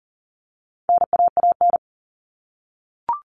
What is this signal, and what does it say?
Morse code, heard as a keyed tone, about 40 wpm, 700 Hz. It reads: NAAN
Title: All Cut Numbers - 4-Digit 40wpm